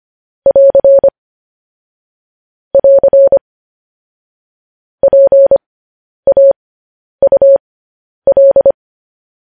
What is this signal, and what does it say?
Morse code, heard as a keyed tone, about 25 wpm, spaced as for 10 wpm, 550 Hz. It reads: <AR> <AR> PAUL
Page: News Headlines in Morse Code at 10 WPM